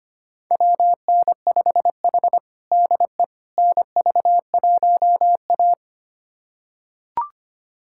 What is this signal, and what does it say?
Morse code, heard as a keyed tone, about 25 wpm, 700 Hz. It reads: WN5H DE NV1A
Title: USA Callsign DE Callsign - 4 Characters - 25wpm